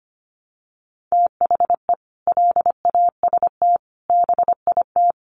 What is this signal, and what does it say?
Morse code, heard as a keyed tone, about 25 wpm, 700 Hz. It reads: THE LAST BIT
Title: Sentences from Top 900 Words 25wpm